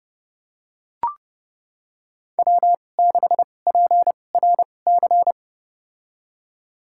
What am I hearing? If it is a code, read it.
Morse code, heard as a keyed tone, about 30 wpm, spaced as for 22 wpm, 700 Hz. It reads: W6PRC
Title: USA Call Signs - 5 Characters - 2x Character Spacing - 30wpm